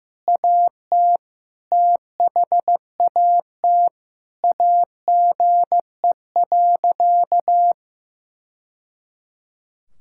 Morse code, heard as a keyed tone, about 15 wpm, 700 Hz. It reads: AT THAT AGE.
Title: Sentences from Top 1000 Words - Review Part 6 - 15wpm